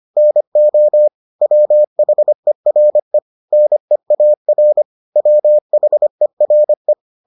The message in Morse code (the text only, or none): NO WHERE NEAR WHERE